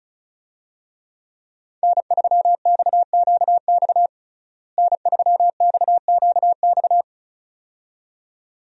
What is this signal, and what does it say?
Morse code, heard as a keyed tone, about 34 wpm, 700 Hz. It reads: N3XQX N3XQX
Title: USA Call Signs - 5 Characters - Repeated 2x - 35wpm